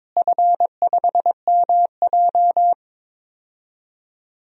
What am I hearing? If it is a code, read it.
Morse code, heard as a keyed tone, about 21 wpm, 700 Hz. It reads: F5MJ
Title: DXCC-100 Call Signs - 4 Characters - Repeated 2x - 22wpm